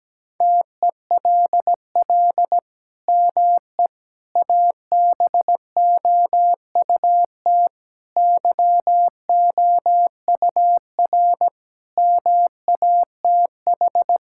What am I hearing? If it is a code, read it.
Morse code, heard as a keyed tone, about 17 wpm, 700 Hz. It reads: TELL ME ABOUT YOUR MATH
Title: Sentences from Top 1000 Words - Review Part 6 - 17wpm